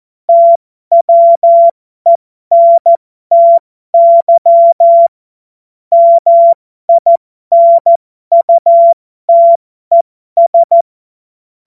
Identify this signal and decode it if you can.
Morse code, heard as a keyed tone, about 14 wpm, spaced as for 12 wpm, 680 Hz. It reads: TWENTY MINUTES
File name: Morsekurs / Witze